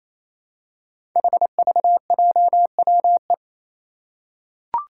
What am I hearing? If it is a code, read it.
Morse code, heard as a keyed tone, about 28 wpm, 700 Hz. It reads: HVJWE
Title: Five Challenging Letter-Number Combinations 28wpm